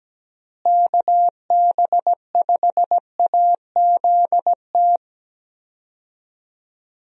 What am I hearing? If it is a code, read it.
Morse code, heard as a keyed tone, about 17 wpm, 700 Hz. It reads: KB5AZT
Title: USA Call Signs - 6 Characters - Repeated 3x - 17wpm